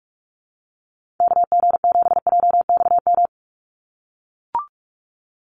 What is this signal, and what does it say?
Morse code, heard as a keyed tone, about 44 wpm, 700 Hz. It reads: KG7JXM
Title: USA Call Signs 45wpm